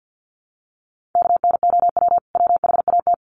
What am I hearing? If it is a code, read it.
Morse code, heard as a keyed tone, about 50 wpm, 700 Hz. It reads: KNOW WHAT
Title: Sets of 2 Words 50wpm